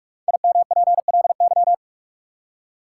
Morse code, heard as a keyed tone, about 44 wpm, 700 Hz. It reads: IMPLY
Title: Five Letter Words - 45wpm